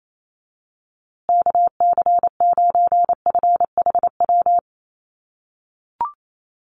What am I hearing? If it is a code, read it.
Morse code, heard as a keyed tone, about 28 wpm, 700 Hz. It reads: KC9FHW
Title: USA Call Signs Repeated 3x 28wpm